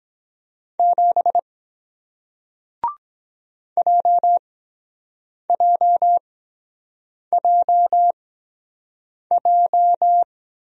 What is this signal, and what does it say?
Morse code, heard as a keyed tone, about 23 wpm, spaced as for 7 wpm, 700 Hz. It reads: 7 JJJJ